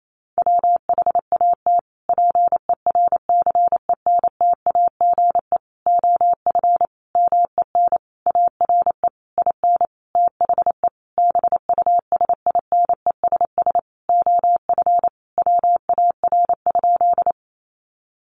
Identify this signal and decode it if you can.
Morse code, heard as a keyed tone, about 28 wpm, 700 Hz. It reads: WHAT PERCENTAGE OF MEN ARE IN THE BUSINESS OF WAR?
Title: Sentences from Top 1300 Words 28wpm